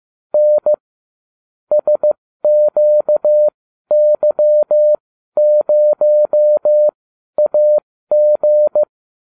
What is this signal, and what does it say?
Morse code, heard as a keyed tone, about 15 wpm, spaced as for 12 wpm, 600 Hz. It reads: N SQY0AG